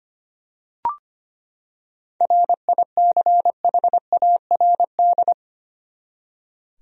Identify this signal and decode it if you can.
Morse code, heard as a keyed tone, about 25 wpm, 700 Hz. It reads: RICHARD
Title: USA Common Names - 25wpm